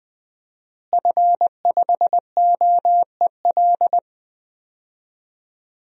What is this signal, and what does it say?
Morse code, heard as a keyed tone, about 20 wpm, 700 Hz. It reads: F5OEL
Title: DXCC-100 Call Signs - 5 Characters - Repeated 2x - 20wpm